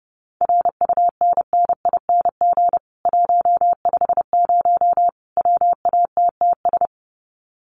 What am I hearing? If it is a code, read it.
Morse code, heard as a keyed tone, about 30 wpm, 700 Hz. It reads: RUNNING 150 WATTS
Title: Common QSO Phrases 30wpm